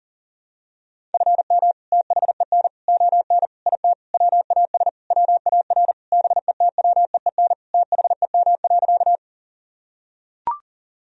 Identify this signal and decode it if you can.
Morse code, heard as a keyed tone, about 40 wpm, 700 Hz. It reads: FM THEN ON IT WAS WAR BETWEEN THEM.
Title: Sentences from Top 400 Words and Abbreviations 40wpm